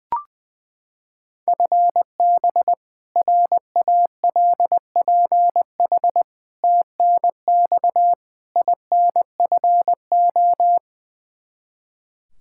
Morse code, heard as a keyed tone, about 20 wpm, 700 Hz. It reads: FB RALPH TNX INFO
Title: Practice QSO - Exchange 3 - More Info - 20wpm